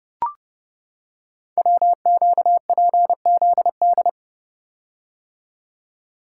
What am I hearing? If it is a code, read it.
Morse code, heard as a keyed tone, about 30 wpm, 700 Hz. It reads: WQPZD
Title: Code Groups - 5 Letter:Numbers - 30wpm